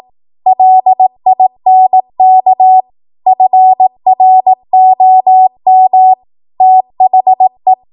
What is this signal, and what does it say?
Morse code, heard as a keyed tone, about 18 wpm, 750 Hz. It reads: LINK FROM THE